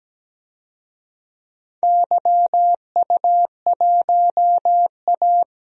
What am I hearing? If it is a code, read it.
Morse code, heard as a keyed tone, about 17 wpm, 700 Hz. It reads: YU1A
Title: DXCC-100 Call Signs - 4 Characters - 17wpm